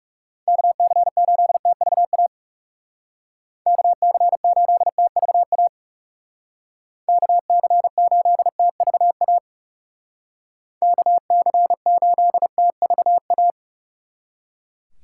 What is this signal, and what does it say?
Morse code, heard as a keyed tone, about 39 wpm, 700 Hz. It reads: KC8TVA KC8TVA KC8TVA KC8TVA